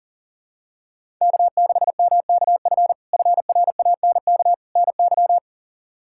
Morse code, heard as a keyed tone, about 40 wpm, 700 Hz. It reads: K6MKF FRANK NY
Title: K1USN SST QSO Practice - 40wpm